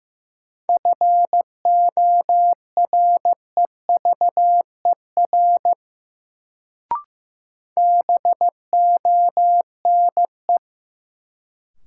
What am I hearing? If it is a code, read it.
Morse code, heard as a keyed tone, about 15 wpm, 700 Hz. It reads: FOREVER BONE
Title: Sentences from Top 1600 Words - New Word Drill - 15wpm